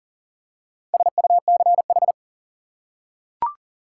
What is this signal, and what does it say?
Morse code, heard as a keyed tone, about 40 wpm, 700 Hz. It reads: SUCH
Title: Four Letter Words - 40wpm